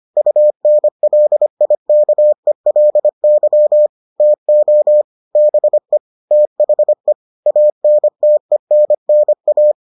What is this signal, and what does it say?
Morse code, heard as a keyed tone, about 25 wpm, 600 Hz. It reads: UNLIKELY TO BE THE ANTENNA